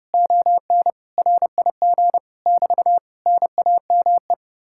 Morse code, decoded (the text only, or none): ON RIG <BT> NAME